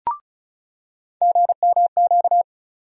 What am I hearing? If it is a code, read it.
Morse code, heard as a keyed tone, about 36 wpm, 700 Hz. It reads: GMQ
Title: USA Call Sign Postfixes - 3 Letters 35wpm